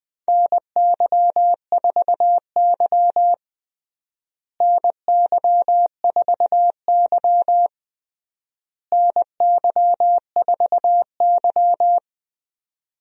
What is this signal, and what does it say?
Morse code, heard as a keyed tone, about 21 wpm, 700 Hz. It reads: NY4Y NY4Y NY4Y
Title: USA Call Signs - 4 Characters - Repeated 3x - 20wpm